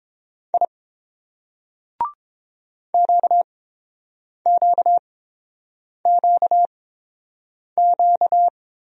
Morse code, heard as a keyed tone, about 30 wpm, spaced as for 7 wpm, 700 Hz. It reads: I QQQQ